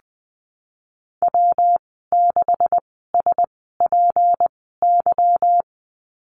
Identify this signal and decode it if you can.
Morse code, heard as a keyed tone, about 20 wpm, spaced as for 15 wpm, 700 Hz. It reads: W6SPY
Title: 173 Lesson 28g - Number 9 - Call Signs 20wpm